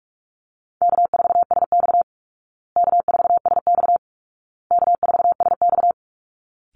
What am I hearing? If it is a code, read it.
Morse code, heard as a keyed tone, about 44 wpm, 700 Hz. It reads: K4SX K4SX K4SX
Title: Call Signs Repeated 2x - 45wpm